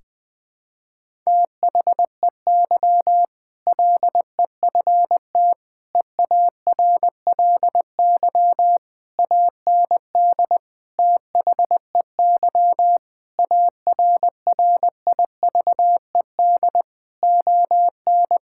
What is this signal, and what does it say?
Morse code, heard as a keyed tone, about 20 wpm, 700 Hz. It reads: THEY LEFT EARLY AND THEY ARRIVED ON
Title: Sentences with 2-Word Phrases - 20wpm